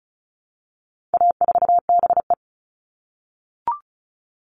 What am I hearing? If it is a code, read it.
Morse code, heard as a keyed tone, about 34 wpm, 700 Hz. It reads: A4BE
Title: All Cut Numbers - 4-Digit 35wpm